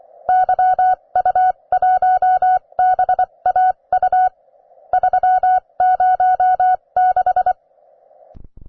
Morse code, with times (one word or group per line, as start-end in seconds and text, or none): 0.3-4.3 YU1BAU
4.9-7.5 306